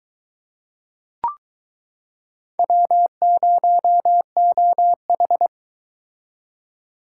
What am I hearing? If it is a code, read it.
Morse code, heard as a keyed tone, about 23 wpm, 700 Hz. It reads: W0OH